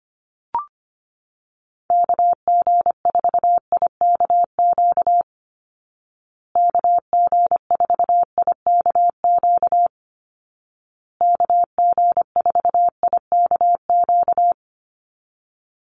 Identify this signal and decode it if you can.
Morse code, heard as a keyed tone, about 25 wpm, 700 Hz. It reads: KG4IKQ KG4IKQ KG4IKQ
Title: Call Signs Repeated 2x - 25wpm